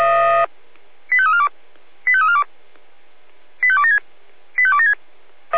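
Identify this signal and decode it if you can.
Five-tone selective call (ZVEI), heard as quick runs of five tones, tones of about 70 ms. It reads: 85342 85342 85276 85276
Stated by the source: Five-tone ZVEI selective call, received with an SDR.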